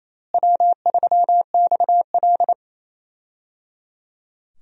Morse code, heard as a keyed tone, about 28 wpm, 700 Hz. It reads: W3XL
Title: USA Call Signs - 4 Characters - Repeated 3x - 28wpm